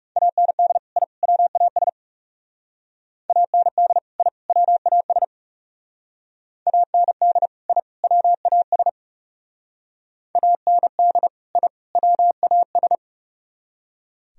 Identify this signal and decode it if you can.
Morse code, heard as a keyed tone, about 36 wpm, 700 Hz. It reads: AND I WAS AND I WAS AND I WAS AND I WAS
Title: Speed-Racing - Sets of 3 Words 30wpm